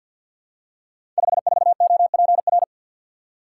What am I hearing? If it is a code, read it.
Morse code, heard as a keyed tone, about 50 wpm, 700 Hz. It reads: 54OPR
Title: Code Groups - 5 Letter:Numbers - 50wpm